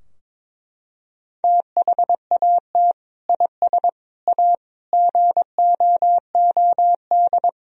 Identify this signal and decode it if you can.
Morse code, heard as a keyed tone, about 22 wpm, 700 Hz. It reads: THAT IS A GOOD
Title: Sentences from Top 1000 Words - Review Part 3 - 22wpm